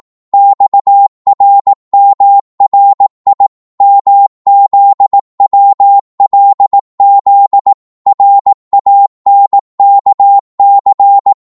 Morse code, heard as a keyed tone, about 18 wpm, 800 Hz. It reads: XRMRI MZWLZ RANKC
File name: Morse / ltr-18wpm